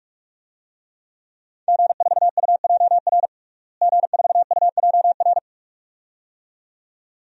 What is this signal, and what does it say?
Morse code, heard as a keyed tone, about 41 wpm, 700 Hz. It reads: G4UJR G4UJR
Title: DXCC-100 Call Signs - 5 Characters - Repeated 2x - 45wpm